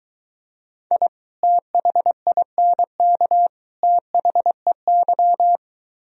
Morse code, heard as a keyed tone, about 23 wpm, 700 Hz. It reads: I THINK THEY